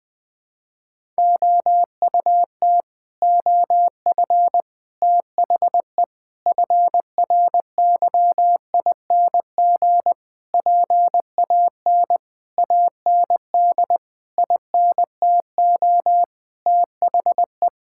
Morse code, heard as a keyed tone, about 20 wpm, 700 Hz. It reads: OUT OF THE FRYING PAN AND INTO THE
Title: Sets of 9 Words - 20wpm